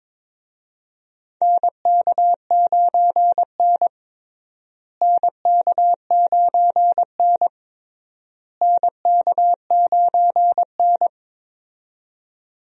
Morse code, heard as a keyed tone, about 21 wpm, 700 Hz. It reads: NK9N NK9N NK9N
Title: USA Call Signs - 4 Characters - Repeated 3x - 22wpm